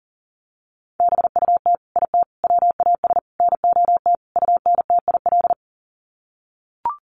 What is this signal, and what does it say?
Morse code, heard as a keyed tone, about 40 wpm, 700 Hz. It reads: BUT IT WAS NOT UNTIL
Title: Sets of 5 Words Encore2 40wpm